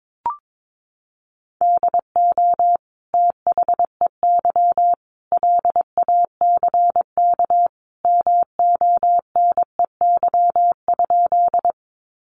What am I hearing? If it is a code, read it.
Morse code, heard as a keyed tone, about 22 wpm, 700 Hz. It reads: DO THEY LACK MONEY?
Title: Sentences from Top 1100 Words 22wpm